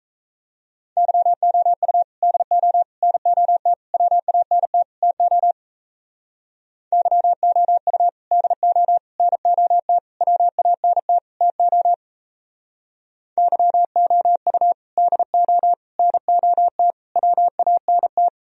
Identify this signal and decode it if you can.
Morse code, heard as a keyed tone, about 36 wpm, 700 Hz. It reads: YOU DO NOT WANT TO YOU DO NOT WANT TO YOU DO NOT WANT